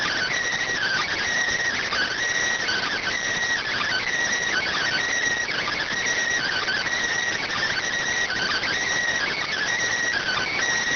Emergency Alert System (SAME) tones (header bursts), heard as harsh buzzing bursts. It is heard from 0.7 to 10.5 s.